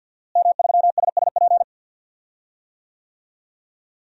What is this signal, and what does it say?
Morse code, heard as a keyed tone, about 51 wpm, 700 Hz. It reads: M3SSP
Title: DXCC-100 Call Signs - 5 Characters - Repeated 3x - 50wpm